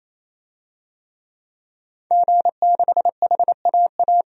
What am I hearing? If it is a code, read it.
Morse code, heard as a keyed tone, about 28 wpm, 700 Hz. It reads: G6HAA
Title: DXCC-100 Call Signs - 5 Characters - 28wpm